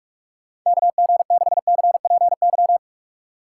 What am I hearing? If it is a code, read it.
Morse code, heard as a keyed tone, about 46 wpm, 700 Hz. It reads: KG6CPY
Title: USA Call Signs - 6 Characters - 2x Character Spacing - 45wpm